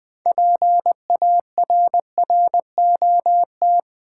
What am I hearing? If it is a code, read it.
Morse code, heard as a keyed tone, about 20 wpm, 690 Hz. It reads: PARROT
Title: Easy Words with Hints - 20wpm